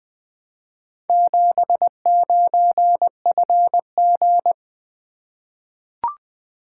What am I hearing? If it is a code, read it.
Morse code, heard as a keyed tone, about 20 wpm, 700 Hz. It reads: 79FG
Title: Four Letter-Number Combinations - 20wpm